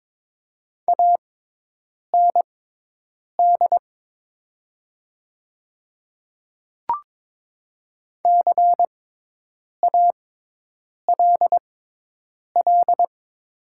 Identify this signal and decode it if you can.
Morse code, heard as a keyed tone, about 22 wpm, spaced as for 8 wpm, 700 Hz. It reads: AND CALL
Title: 6x Character Spacing - Top 100 Words - 22wpm